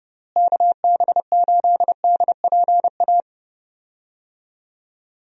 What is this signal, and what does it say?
Morse code, heard as a keyed tone, about 31 wpm, 700 Hz. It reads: KB8DPA
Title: USA Call Signs - 6 Characters - Repeated 2x - 30wpm